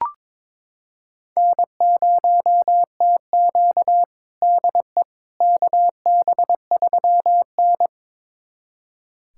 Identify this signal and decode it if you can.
Morse code, heard as a keyed tone, about 22 wpm, 700 Hz. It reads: N0TQ DE KB3N
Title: USA Callsign DE Callsign - 4 Characters - 22wpm